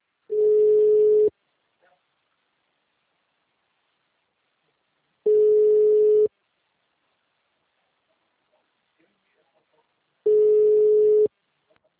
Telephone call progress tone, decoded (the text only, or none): ringback tone